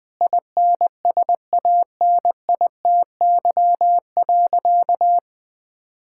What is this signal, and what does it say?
Morse code, heard as a keyed tone, about 20 wpm, 700 Hz. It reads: INSANITY.
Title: Sentences with 8-Word Phrases - Spoken Beginning - 20wpm